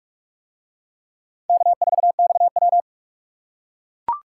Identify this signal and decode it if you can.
Morse code, heard as a keyed tone, about 39 wpm, 700 Hz. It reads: K4XW